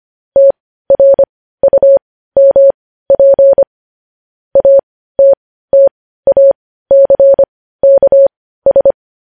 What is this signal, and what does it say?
Morse code, heard as a keyed tone, about 25 wpm, spaced as for 15 wpm, 550 Hz. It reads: TRUMP ATTACKS
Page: News headlines in morse code at 15 wpm